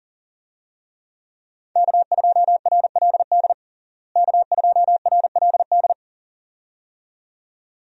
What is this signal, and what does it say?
Morse code, heard as a keyed tone, about 40 wpm, 700 Hz. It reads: K2RLD K2RLD
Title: USA Call Signs - 5 Characters - Repeated 2x - 40wpm